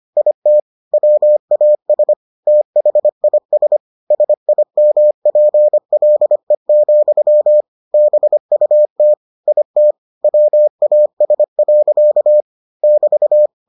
Morse code, heard as a keyed tone, about 25 wpm, 600 Hz. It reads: IT WAS THIS SIMPLE, BUT IT WAS. <BT>